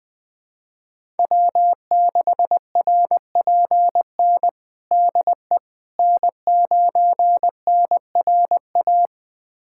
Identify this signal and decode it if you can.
Morse code, heard as a keyed tone, about 20 wpm, 700 Hz. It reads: W6RPN DE N9NRA
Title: USA Callsign DE Callsign - 5 Characters - 20wpm